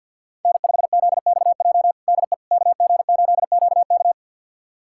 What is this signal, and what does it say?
Morse code, heard as a keyed tone, about 47 wpm, 700 Hz. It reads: N5ZXJ DE KG8QK
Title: USA Callsign DE Callsign - 5 Characters - 50wpm